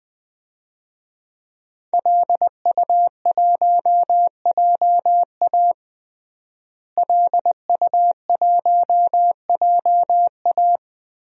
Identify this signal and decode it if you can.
Morse code, heard as a keyed tone, about 20 wpm, 700 Hz. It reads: LU1JA LU1JA
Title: DXCC-100 Call Signs - 5 Characters - Repeated 3x - 20wpm